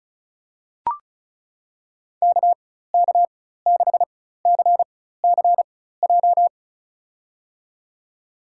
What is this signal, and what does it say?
Morse code, heard as a keyed tone, about 36 wpm, spaced as for 16 wpm, 700 Hz. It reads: KK6CCJ
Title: USA Call Signs - 6 Characters - 4x Character Spacing - 35wpm